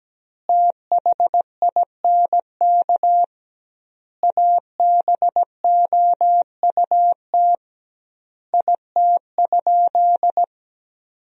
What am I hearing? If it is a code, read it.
Morse code, heard as a keyed tone, about 17 wpm, 700 Hz. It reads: THINK ABOUT IT?
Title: 2x Word-Spacing - Sentences from Top 200 Words - 17wpm